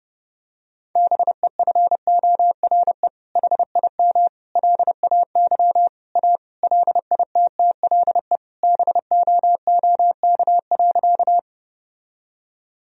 Morse code, heard as a keyed tone, about 30 wpm, 700 Hz. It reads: BEFORE HIM LAY A LITTLE BOOK.
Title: Sentences from Top 1000 Words - Review Part 2 - 30wpm